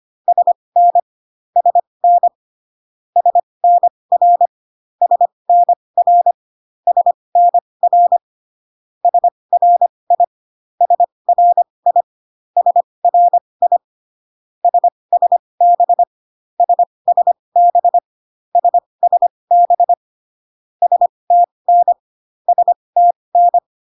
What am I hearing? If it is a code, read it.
Morse code, heard as a keyed tone, about 25 wpm, spaced as for 20 wpm, 700 Hz. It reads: SN SN SNR SNR SNR SRI SRI SRI SSB SSB SSB STN STN